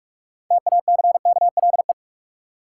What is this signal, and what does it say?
Morse code, heard as a keyed tone, about 46 wpm, 700 Hz. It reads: TACKLE
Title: Six Letter Words - 45wpm